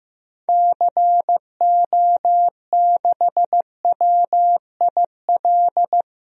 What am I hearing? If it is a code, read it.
Morse code, heard as a keyed tone, about 15 wpm, 700 Hz. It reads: CO6WIL
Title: DXCC-100 Call Signs - 6 Characters - Repeated 2x - 15wpm